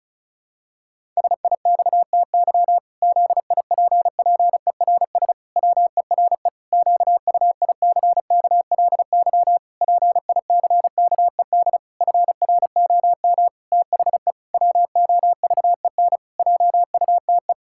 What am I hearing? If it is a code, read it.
Morse code, heard as a keyed tone, about 36 wpm, 700 Hz. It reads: SIXTY ZIPPERS WERE QUICKLY PICKED FROM THE WOVEN JUTE
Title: Pangram Challenge v2 - 35wpm